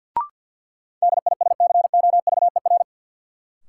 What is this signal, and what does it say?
Morse code, heard as a keyed tone, about 52 wpm, 700 Hz. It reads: DISCOVER
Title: Sentences from Top 1100 Words - New Word Drill Alt - 50wpm